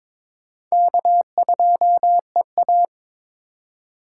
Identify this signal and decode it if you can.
Morse code, heard as a keyed tone, about 22 wpm, 700 Hz. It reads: K2EA
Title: USA Call Signs - 4 Characters - 4x Character Spacing - 22wpm